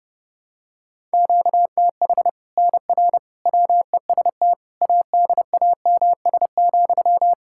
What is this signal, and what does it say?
Morse code, heard as a keyed tone, about 30 wpm, 700 Hz. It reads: QTH NR WEST ADAMS,
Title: Practice QSO - Exchange 1 - Beginning - 30wpm